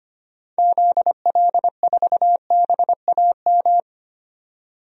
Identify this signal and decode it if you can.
Morse code, heard as a keyed tone, about 25 wpm, 700 Hz. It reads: ZL4BAM
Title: DXCC-100 Call Signs - 6 Characters - 25wpm